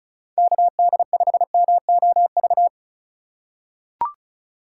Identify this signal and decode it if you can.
Morse code, heard as a keyed tone, about 34 wpm, 700 Hz. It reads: KD5MOV
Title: USA Call Signs - 6 Characters - 2x Character Spacing - 35wpm